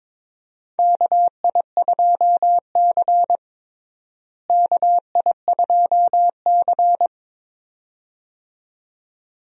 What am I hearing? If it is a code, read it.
Morse code, heard as a keyed tone, about 21 wpm, 700 Hz. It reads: KI2C KI2C
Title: USA Call Signs - 4 Characters - Repeated 2x - 22wpm